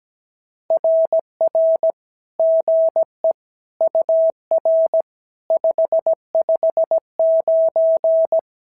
Morse code, heard as a keyed tone, about 17 wpm, 650 Hz. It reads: RR GE UR 559
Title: SOTA QSO Practice 17wpm